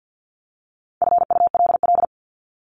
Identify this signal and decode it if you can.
Morse code, heard as a keyed tone, about 48 wpm, 700 Hz. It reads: FULL
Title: LY Suffix - Top 100 Words with Base Words 50wpm